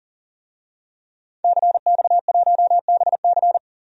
Morse code, heard as a keyed tone, about 40 wpm, 700 Hz. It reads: CX1BC
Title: DXCC-100 Call Signs - 5 Characters - Repeated 2x - 40wpm